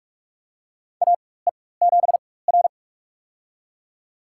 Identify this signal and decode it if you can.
Morse code, heard as a keyed tone, about 43 wpm, spaced as for 21 wpm, 700 Hz. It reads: AE7R
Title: USA Call Signs - 4 Characters - 4x Character Spacing - 45wpm